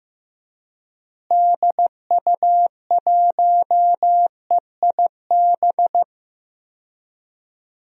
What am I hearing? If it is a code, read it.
Morse code, heard as a keyed tone, about 15 wpm, 700 Hz. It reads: DU1EIB
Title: DXCC-100 Call Signs - 6 Characters - Repeated 3x - 15wpm